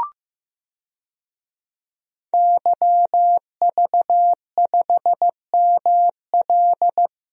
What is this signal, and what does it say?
Morse code, heard as a keyed tone, about 16 wpm, 700 Hz. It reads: YV5ML
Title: DXCC-100 Call Signs - 5 Characters - Repeated 3x - 15wpm